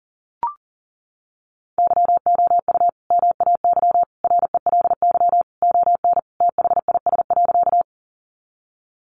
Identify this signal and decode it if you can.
Morse code, heard as a keyed tone, about 40 wpm, 700 Hz. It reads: YOU MAY RELY ON THIS.
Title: Sentences from Top 1200 Words 40wpm